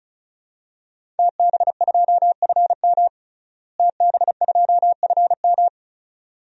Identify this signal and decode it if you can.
Morse code, heard as a keyed tone, about 34 wpm, 700 Hz. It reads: TB2FM TB2FM
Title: DXCC-100 Call Signs - 5 Characters - Repeated 2x - 35wpm